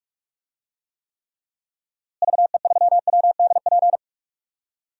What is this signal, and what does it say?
Morse code, heard as a keyed tone, about 46 wpm, 700 Hz. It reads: VE3WDP
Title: DXCC-100 Call Signs - 6 Characters - Repeated 3x - 45wpm